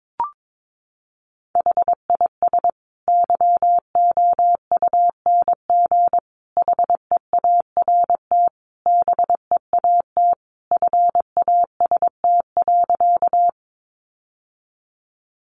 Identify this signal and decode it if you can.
Morse code, heard as a keyed tone, about 22 wpm, 700 Hz. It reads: HIS YOUNG HEART BEAT FAST.
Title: Sentences from Top 1300 Words 22wpm